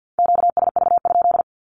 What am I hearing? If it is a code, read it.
Morse code, heard as a keyed tone, about 50 wpm, 700 Hz. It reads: QSV?
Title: Q-Codes Repeated 3x 50wpm